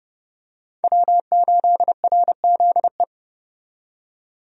Morse code, heard as a keyed tone, about 30 wpm, 700 Hz. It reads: W8RZE
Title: USA Call Signs - 5 Characters - 30wpm